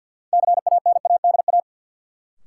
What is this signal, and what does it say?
Morse code, heard as a keyed tone, about 47 wpm, 700 Hz. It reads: CANADA
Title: Sets of 5 Words - New Word Drill - 50wpm